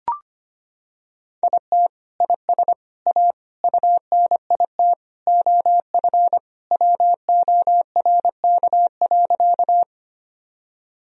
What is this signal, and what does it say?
Morse code, heard as a keyed tone, about 25 wpm, 700 Hz. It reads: IT IS A UNIT OF WORK.
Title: Sentences from Top 1000 Words - Review Part 2 - 25wpm